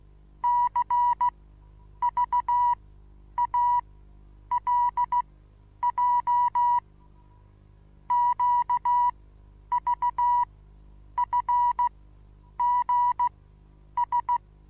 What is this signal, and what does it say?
Morse code, heard as a keyed tone, about 16 wpm, spaced as for 9 wpm, 970 Hz. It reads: CVALJ QVFGS
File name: Onlinelehrgang / morsen / DJ4UF Morsen25